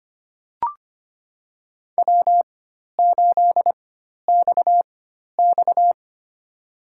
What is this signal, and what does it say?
Morse code, heard as a keyed tone, about 25 wpm, spaced as for 12 wpm, 700 Hz. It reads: W8XX
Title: USA Call Signs - 4 Characters - 4x Character Spacing - 25wpm